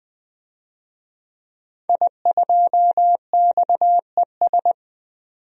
Morse code, heard as a keyed tone, about 20 wpm, 700 Hz. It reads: I2XES
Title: DXCC-100 Call Signs - 5 Characters - 20wpm